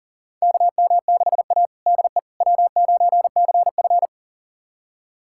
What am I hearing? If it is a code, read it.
Morse code, heard as a keyed tone, about 40 wpm, 700 Hz. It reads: KM6A DE W9CF
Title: USA Callsign DE Callsign - 4 Characters - 40wpm